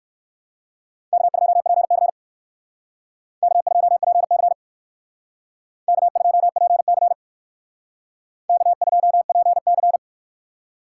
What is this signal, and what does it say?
Morse code, heard as a keyed tone, about 57 wpm, 700 Hz. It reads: K2PC K2PC K2PC K2PC